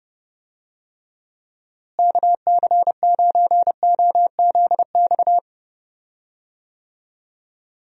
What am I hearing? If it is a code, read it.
Morse code, heard as a keyed tone, about 32 wpm, 700 Hz. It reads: KC9OZX
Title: USA Call Signs - 6 Characters - Repeated 2x - 30wpm